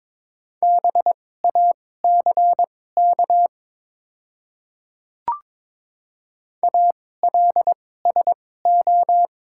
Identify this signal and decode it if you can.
Morse code, heard as a keyed tone, about 22 wpm, spaced as for 16 wpm, 700 Hz. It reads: BACK ALSO
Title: 2x Character Spacing - Top 200 Words - 22wpm